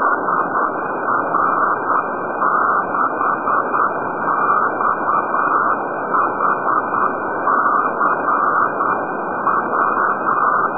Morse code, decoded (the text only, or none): none